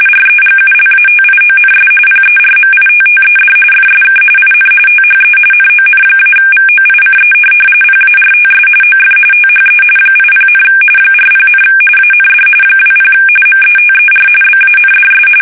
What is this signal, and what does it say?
RTTY, about 45.45 baud. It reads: REGULATIONS PERMITTED FSK IN THE NON-VOICE PARTS OF THE 80, 40, AND 20 METER BANDS
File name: fsk850 / teletype-RTTY